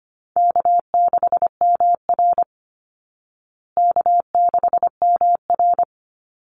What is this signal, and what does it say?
Morse code, heard as a keyed tone, about 25 wpm, 700 Hz. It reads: K6MR K6MR
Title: Easy Call Signs Repeated 3x - 25wpm